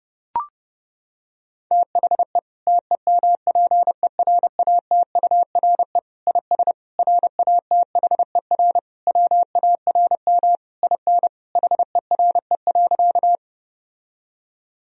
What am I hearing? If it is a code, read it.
Morse code, heard as a keyed tone, about 32 wpm, 700 Hz. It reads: THE TEMPERATURE IS RATHER WARM IN HERE.
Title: Sentences from Top 1000 Words - Review Part 4 - 30wpm